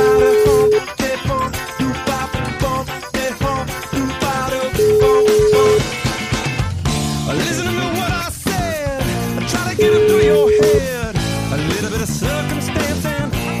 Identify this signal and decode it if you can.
Ringback tone (Europe, 425 Hz), with noise.